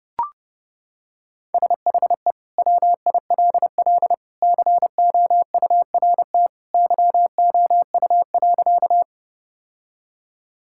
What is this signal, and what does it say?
Morse code, heard as a keyed tone, about 30 wpm, 700 Hz. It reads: SHE WILL COURT YOU.
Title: Sentences from Top 1600 Words 30wpm